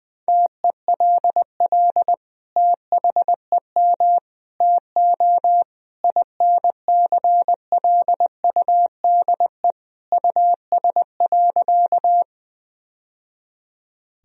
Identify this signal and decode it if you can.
Morse code, heard as a keyed tone, about 20 wpm, 700 Hz. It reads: TELL THEM TO INCLUDE US.
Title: Sentences from Top 1000 Words - Review Part 3 - 20wpm